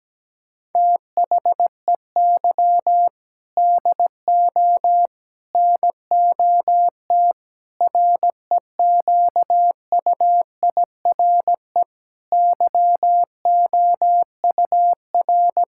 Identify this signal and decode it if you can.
Morse code, heard as a keyed tone, about 17 wpm, 700 Hz. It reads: THEY DO NOT REQUIRE YOUR
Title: Sentences from Top 1000 Words - Review Part 6 - 17wpm